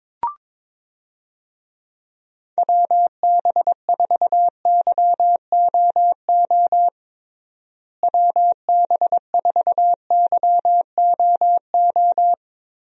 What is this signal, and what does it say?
Morse code, heard as a keyed tone, about 22 wpm, 700 Hz. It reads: WB4YOO WB4YOO
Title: USA Call Signs - 6 Characters - Repeated 3x - 22wpm